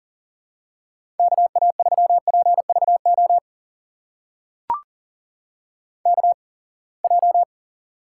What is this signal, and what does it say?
Morse code, heard as a keyed tone, about 42 wpm, spaced as for 28 wpm, 700 Hz. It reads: KA3PVO KJ
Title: USA Call Signs - 6 Characters - 8x Character Spacing - 40wpm